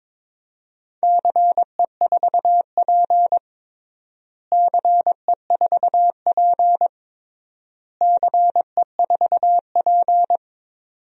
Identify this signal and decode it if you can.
Morse code, heard as a keyed tone, about 22 wpm, 700 Hz. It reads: CE4P CE4P CE4P
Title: DXCC-100 Call Signs - 4 Characters - Repeated 3x - 22wpm